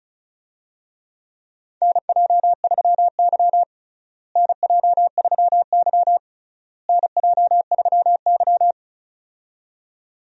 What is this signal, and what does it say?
Morse code, heard as a keyed tone, about 36 wpm, 700 Hz. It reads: NJ3Y NJ3Y NJ3Y
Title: USA Call Signs - 4 Characters - Repeated 3x - 35wpm